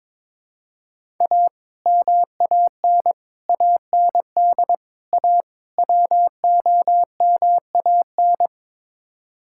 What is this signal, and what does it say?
Morse code, heard as a keyed tone, about 21 wpm, 700 Hz. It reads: A MAN AND A WOMAN
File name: practice / 22wpm / Build A Phrase - 5-Word Phrases - 22wpm